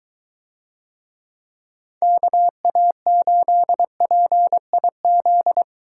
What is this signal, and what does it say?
Morse code, heard as a keyed tone, about 21 wpm, 700 Hz. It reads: KA8PIZ